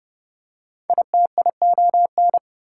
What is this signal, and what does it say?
Morse code, heard as a keyed tone, about 30 wpm, 700 Hz. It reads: ITION
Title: Prefixes and Suffixes 30wpm